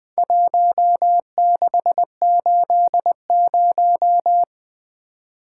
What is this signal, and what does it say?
Morse code, heard as a keyed tone, about 20 wpm, 700 Hz. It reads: 1680
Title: Four Letter-Number Combinations - 20wpm